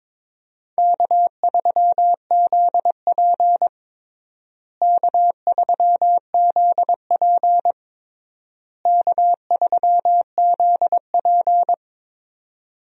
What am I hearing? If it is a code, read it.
Morse code, heard as a keyed tone, about 22 wpm, 700 Hz. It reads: K3ZP K3ZP K3ZP
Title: USA Call Signs - 4 Characters - Repeated 3x - 22wpm